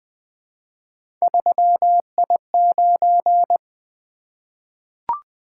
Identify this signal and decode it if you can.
Morse code, heard as a keyed tone, about 20 wpm, 700 Hz. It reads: 3I9
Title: Three Letter-Number Combinations - 20wpm